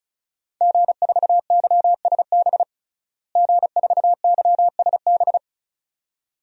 Morse code, heard as a keyed tone, about 36 wpm, 700 Hz. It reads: G4YSB G4YSB
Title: DXCC-100 Call Signs - 5 Characters - Repeated 2x - 35wpm